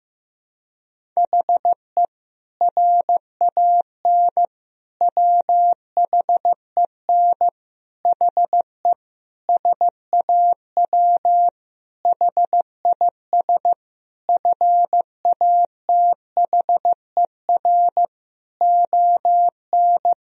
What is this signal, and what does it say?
Morse code, heard as a keyed tone, about 15 wpm, 700 Hz. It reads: HE RAN WHEN HE SAW HIS FATHER ON
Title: Sentences from Top 500 Words and Abbreviations 15wpm